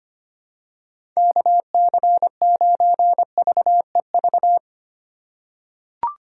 Morse code, heard as a keyed tone, about 25 wpm, 700 Hz. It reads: KC9VEV
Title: USA Call Signs - 6 Characters - 4x Character Spacing - 25wpm